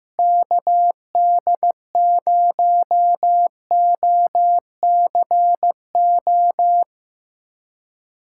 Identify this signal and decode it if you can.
Morse code, heard as a keyed tone, about 15 wpm, 700 Hz. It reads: KD0OCO
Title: USA Callsign DE Callsign - 6 Characters - 15wpm